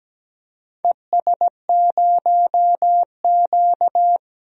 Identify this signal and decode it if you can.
Morse code, heard as a keyed tone, about 17 wpm, 700 Hz. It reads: ES0Q
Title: DXCC-100 Call Signs - 4 Characters - 17wpm